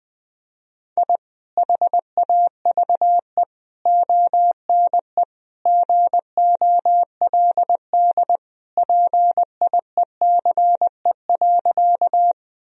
Morse code, heard as a keyed tone, about 21 wpm, 700 Hz. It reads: I HAVE ONE GOLD PIECE.
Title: Sentences from Top 1000 Words - Review Part 3 - 20wpm